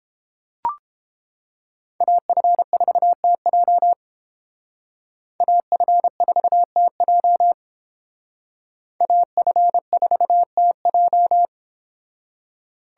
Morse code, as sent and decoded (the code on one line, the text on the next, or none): .- ..-. ....- - .--- / .- ..-. ....- - .--- / .- ..-. ....- - .---
AF4TJ AF4TJ AF4TJ